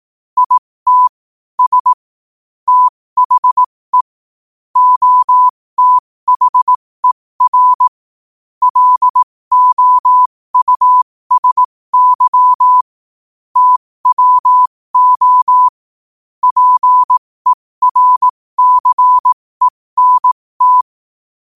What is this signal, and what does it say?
Morse code, heard as a keyed tone, about 18 wpm, spaced as for 15 wpm, 1000 Hz. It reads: IT S THE OTHER LOUSY TWO PERCENT
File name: qotdfiles / 15wpm